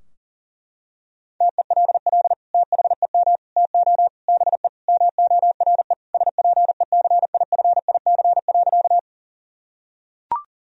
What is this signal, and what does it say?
Morse code, heard as a keyed tone, about 37 wpm, 700 Hz. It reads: TELL THEM TO BE MORE SPECIFIC.
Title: Sentences from Top 1000 Words - Review Part 3 - 40wpm